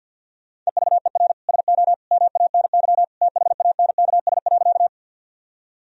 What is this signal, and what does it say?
Morse code, heard as a keyed tone, about 50 wpm, 700 Hz. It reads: EVER SO MANY THANKS.
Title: Sentences from Top 1000 Words - Review Part 3 - 50wpm